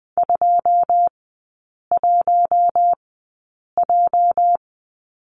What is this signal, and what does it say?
Morse code, heard as a keyed tone, about 20 wpm, spaced as for 8 wpm, 700 Hz. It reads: 21J
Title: 205 Lesson 33e - Letter j - Three Characters 20wpm